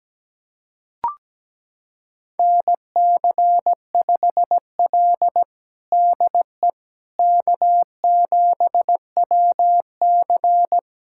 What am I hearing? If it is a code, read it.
Morse code, heard as a keyed tone, about 17 wpm, 700 Hz. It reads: NC5L DE K7WC
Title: USA Callsign DE Callsign - 4 Characters - 17wpm